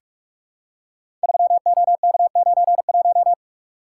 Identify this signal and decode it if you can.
Morse code, heard as a keyed tone, about 41 wpm, 700 Hz. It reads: 3OK91
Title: Code Groups - 5 Letter:Numbers - 45wpm